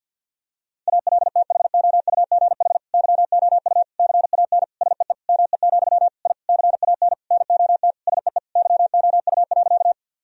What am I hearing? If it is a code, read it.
Morse code, heard as a keyed tone, about 51 wpm, 700 Hz. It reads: ALTHOUGH YOU CAN SEE ME, I CAN NOT SEE YOU.
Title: Sentences from Top 1000 Words - Review Part 3 - 50wpm